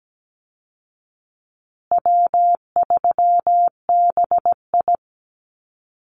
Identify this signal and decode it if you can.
Morse code, heard as a keyed tone, about 17 wpm, 700 Hz. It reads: W3BI
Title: Easy Call Signs Repeated 3x - 17wpm